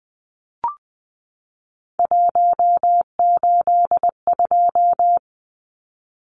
Morse code, heard as a keyed tone, about 20 wpm, 700 Hz. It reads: 182
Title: Numbers - 3-Digit 20wpm